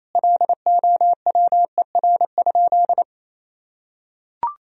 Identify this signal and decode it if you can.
Morse code, heard as a keyed tone, about 27 wpm, 700 Hz. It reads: LOWER?
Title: Sentences from Top 1000 Words - Review Part 4 - 28wpm